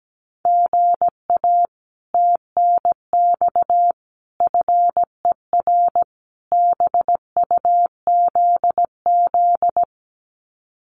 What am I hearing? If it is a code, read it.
Morse code, heard as a keyed tone, about 17 wpm, 700 Hz. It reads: GA TNX FER BUZZ
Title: Common QSO Phrases Encore 17wpm